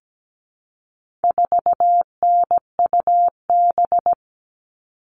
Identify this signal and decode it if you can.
Morse code, heard as a keyed tone, about 17 wpm, 700 Hz. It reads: 4NUB
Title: All Cut Numbers - 4-Digit 17wpm